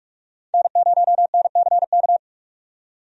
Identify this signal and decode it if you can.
Morse code, heard as a keyed tone, about 46 wpm, 700 Hz. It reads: N0NCK
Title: USA Call Signs - 5 Characters - 45wpm